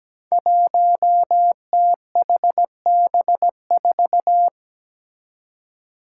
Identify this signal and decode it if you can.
Morse code, heard as a keyed tone, about 17 wpm, 700 Hz. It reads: 1THB4
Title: Five Challenging Letter-Number Combinations 17wpm